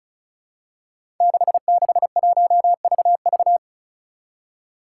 Morse code, heard as a keyed tone, about 36 wpm, 700 Hz. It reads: 661VV
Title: Five Challenging Letter-Number Combinations 35wpm